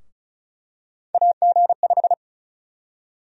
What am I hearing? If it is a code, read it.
Morse code, heard as a keyed tone, about 34 wpm, 700 Hz. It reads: AG5
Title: USA Call Sign Prefixes - Repeated 2x - 35wpm